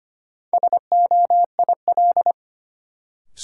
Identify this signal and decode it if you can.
Morse code, heard as a keyed tone, about 25 wpm, 700 Hz. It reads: SOIL
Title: Sentences from Top 1000 Words - Review Part 4 - New Word Drill - 25wpm